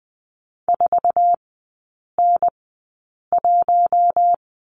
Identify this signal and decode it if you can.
Morse code, heard as a keyed tone, about 20 wpm, spaced as for 8 wpm, 700 Hz. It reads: 4N1
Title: 053 Lesson 10e - Letter r - Three Characters 20wpm